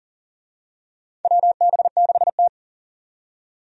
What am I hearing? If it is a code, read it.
Morse code, heard as a keyed tone, about 39 wpm, 700 Hz. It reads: WB6T
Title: USA Call Signs - 4 Characters - 40wpm